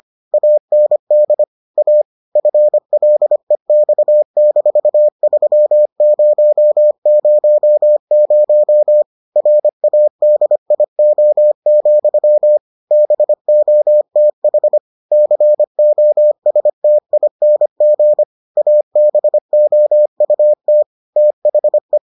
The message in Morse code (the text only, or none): AND A FLEX-3000 RADIO, BOTH COSTING ABOUT THE